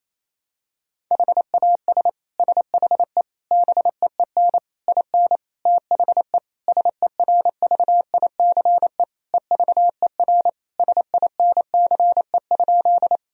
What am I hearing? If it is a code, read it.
Morse code, heard as a keyed tone, about 28 wpm, 700 Hz. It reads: HAS SHE BEEN IN THE SERVICE EVER SINCE?
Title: Sentences from Top 1000 Words - Review Part 4 - 28wpm